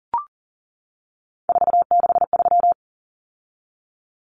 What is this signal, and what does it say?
Morse code, heard as a keyed tone, about 40 wpm, 700 Hz. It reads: 463
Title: Common Cut Numbers - 3-Digit 40wpm